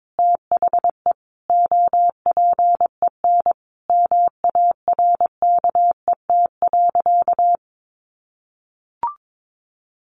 Morse code, heard as a keyed tone, about 22 wpm, 700 Hz. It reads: THE OPEN MARKET.
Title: Sentences from Top 1400 Words 22wpm